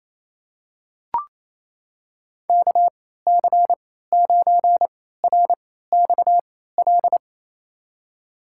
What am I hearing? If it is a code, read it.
Morse code, heard as a keyed tone, about 27 wpm, spaced as for 16 wpm, 700 Hz. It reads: KC9RXL
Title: USA Call Signs - 6 Characters - 3x Character Spacing - 28wpm